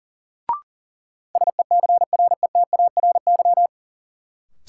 Morse code, heard as a keyed tone, about 39 wpm, 700 Hz. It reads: SECRETARY
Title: Sentences from Top 1300 Words - New Word Drill Alt - 40wpm